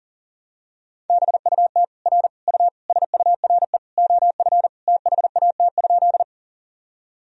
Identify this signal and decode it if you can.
Morse code, heard as a keyed tone, about 40 wpm, 700 Hz. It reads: BUT R U SURE OF THAT?
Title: Sentences from Top 300 Words and Abbreviations 40wpm